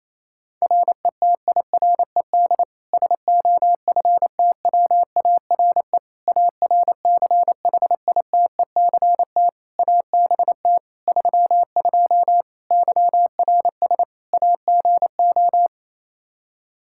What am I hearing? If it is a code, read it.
Morse code, heard as a keyed tone, about 28 wpm, 700 Hz. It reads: RETIRED SOFTWARE ARCHITECT ABT 32 YRS AGO
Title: Practice QSO - Exchange 3 - More Info - 28wpm